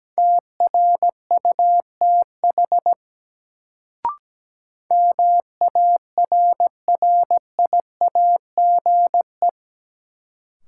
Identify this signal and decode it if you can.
Morse code, heard as a keyed tone, about 17 wpm, 700 Hz. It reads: TRUTH MARRIAGE
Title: Sentences from Top 800 Words - New Word Drill - 17wpm